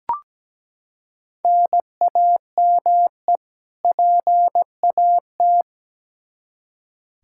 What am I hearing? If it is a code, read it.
Morse code, heard as a keyed tone, about 17 wpm, 700 Hz. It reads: NAME PAT
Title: Practice QSO - Exchange 1 - Beginning - 17wpm